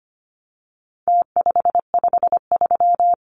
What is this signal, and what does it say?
Morse code, heard as a keyed tone, about 25 wpm, 700 Hz. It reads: T553
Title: Numbers - Contest Style - 4-Digit 25wpm